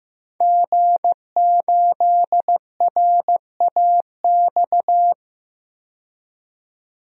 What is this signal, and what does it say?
Morse code, heard as a keyed tone, about 15 wpm, 700 Hz. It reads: G8RAX
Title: DXCC-100 Call Signs - 5 Characters - 15wpm